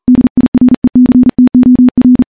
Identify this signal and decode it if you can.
Morse code, heard as a keyed tone, about 39 wpm, 250 Hz. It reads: DIRECTOR
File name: morse / morse-training-40WPM